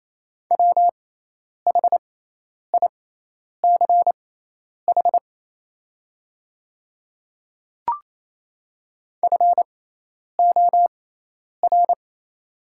Morse code, heard as a keyed tone, about 28 wpm, spaced as for 10 wpm, 700 Hz. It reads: WHICH FOR
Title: 6x Character Spacing - Top 200 Words - 28wpm